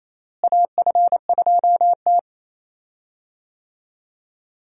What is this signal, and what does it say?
Morse code, heard as a keyed tone, about 28 wpm, 700 Hz. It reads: AF2T
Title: USA Call Signs - 4 Characters - Repeated 3x - 28wpm